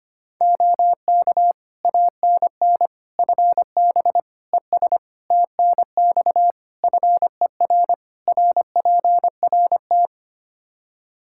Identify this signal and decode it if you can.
Morse code, heard as a keyed tone, about 25 wpm, 700 Hz. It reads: OK ANN FB ES TNX FER RPRT
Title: K1USN SST QSO Practice 25wpm